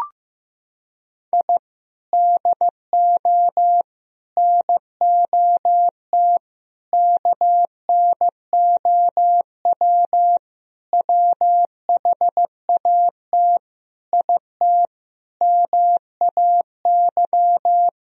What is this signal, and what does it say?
Morse code, heard as a keyed tone, about 15 wpm, 700 Hz. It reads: I DO NOT KNOW WHAT IT MAY
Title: Sentences from Top 1000 Words - Review Part 2 - 15wpm